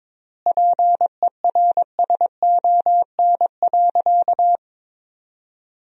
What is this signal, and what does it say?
Morse code, heard as a keyed tone, about 22 wpm, 700 Hz. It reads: PERSON.
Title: 4x Word-Spacing - Sentences from Top 500 Words - 22wpm